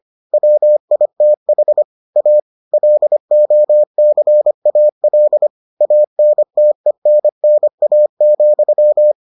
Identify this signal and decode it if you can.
Morse code, heard as a keyed tone, about 25 wpm, 600 Hz. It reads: WITH A LOCAL ANTENNA,